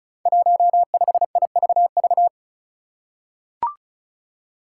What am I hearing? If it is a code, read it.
Morse code, heard as a keyed tone, about 36 wpm, 700 Hz. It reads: 15IVV
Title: Five Challenging Letter-Number Combinations 35wpm